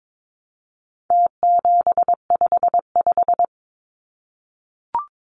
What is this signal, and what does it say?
Morse code, heard as a keyed tone, about 22 wpm, 700 Hz. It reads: T755
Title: Numbers - Contest Style - 4-Digit 22wpm